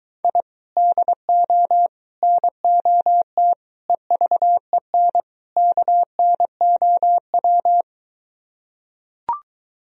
Morse code, heard as a keyed tone, about 23 wpm, 700 Hz. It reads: I DO NOT EVEN KNOW